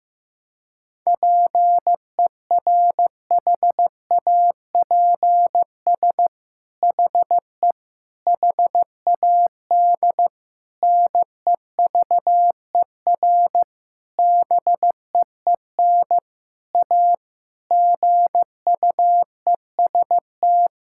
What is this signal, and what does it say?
Morse code, heard as a keyed tone, about 16 wpm, 700 Hz. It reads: PERHAPS HE HAD NEVER BEEN A GUEST
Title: Sentences from Top 1600 Words 15wpm